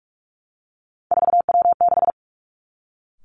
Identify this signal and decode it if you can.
Morse code, heard as a keyed tone, about 46 wpm, 700 Hz. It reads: 4W6
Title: Three Challenging Letter-Number Combinations 45wpm